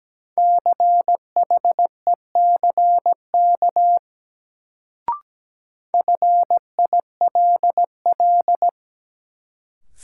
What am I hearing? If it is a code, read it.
Morse code, heard as a keyed tone, about 17 wpm, 700 Hz. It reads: CHECK FILL
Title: Sentences from Top 1000 Words - Review Part 3 - New Word Drill - 17wpm